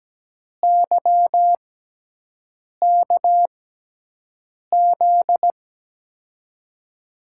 Morse code, heard as a keyed tone, about 17 wpm, spaced as for 6 wpm, 700 Hz. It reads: YKZ
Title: USA Call Signs - 6x Character Spacing - 17wpm